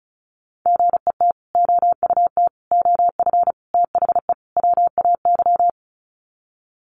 Morse code, heard as a keyed tone, about 34 wpm, 700 Hz. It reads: GET OUT OF THE WAY
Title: Sets of 5 Words Encore 35wpm